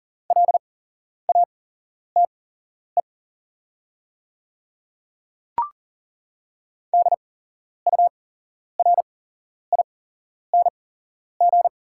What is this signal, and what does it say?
Morse code, heard as a keyed tone, about 40 wpm, spaced as for 11 wpm, 690 Hz. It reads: LATE DURING
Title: 8x Character Spacing - Top 500 Words - 40wpm